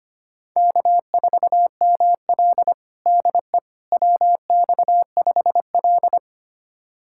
Morse code, heard as a keyed tone, about 25 wpm, 700 Hz. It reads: K4ML DE WX5L
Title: USA Callsign DE Callsign - 4 Characters - 25wpm